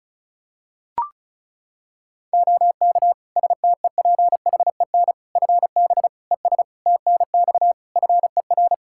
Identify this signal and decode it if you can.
Morse code, heard as a keyed tone, about 36 wpm, 700 Hz. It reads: OK STEPHEN FB ES TNX FER
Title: K1USN SST QSO Practice 35wpm